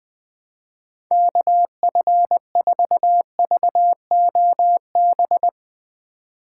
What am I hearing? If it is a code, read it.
Morse code, heard as a keyed tone, about 20 wpm, 700 Hz. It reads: KF4VOB
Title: USA Call Signs - 6 Characters - 2x Character Spacing - 20wpm